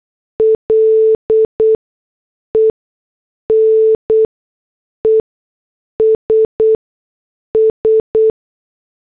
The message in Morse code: LENESS